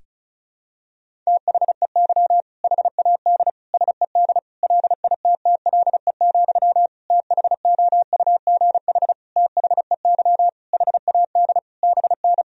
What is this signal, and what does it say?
Morse code, heard as a keyed tone, about 36 wpm, 700 Hz. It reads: THEY HAD SED LITTLE, THOUGH THEY HAD BN
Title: Sentences from Top 400 Words and Abbreviations 35wpm